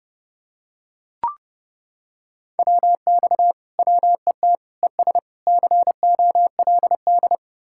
Morse code, heard as a keyed tone, about 28 wpm, 700 Hz. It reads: WX WET ES COLD